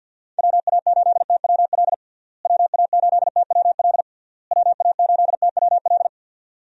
Morse code, heard as a keyed tone, about 47 wpm, 700 Hz. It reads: WA8TWL WA8TWL WA8TWL
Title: USA Call Signs - 6 Characters - Repeated 3x - 50wpm